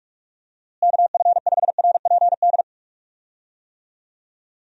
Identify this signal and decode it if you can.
Morse code, heard as a keyed tone, about 44 wpm, 700 Hz. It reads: KF5RPD
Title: USA Call Signs - 6 Characters - Repeated 3x - 45wpm